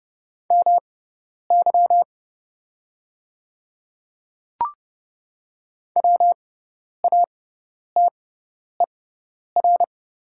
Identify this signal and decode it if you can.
Morse code, heard as a keyed tone, about 30 wpm, spaced as for 10 wpm, 700 Hz. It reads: MY WATER
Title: 6x Character Spacing - Top 100 Words - 30wpm